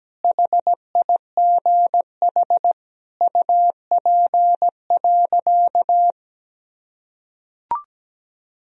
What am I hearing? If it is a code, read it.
Morse code, heard as a keyed tone, about 17 wpm, 700 Hz. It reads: HIGH UP.
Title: Sentences from Top 1000 Words - Review Part 1 - 17wpm